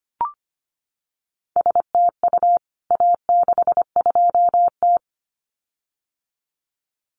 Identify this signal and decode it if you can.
Morse code, heard as a keyed tone, about 25 wpm, 700 Hz. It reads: STU A62T
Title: CWT Exchange 25wpm